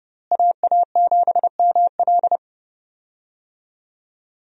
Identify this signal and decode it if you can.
Morse code, heard as a keyed tone, about 30 wpm, 700 Hz. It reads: AA7ML
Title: USA Call Signs - 5 Characters - Repeated 3x - 30wpm